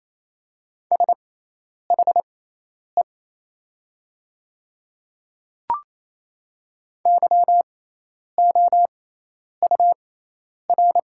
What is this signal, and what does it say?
Morse code, heard as a keyed tone, about 28 wpm, spaced as for 10 wpm, 700 Hz. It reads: SHE YOUR
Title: 6x Character Spacing - Top 100 Words - 28wpm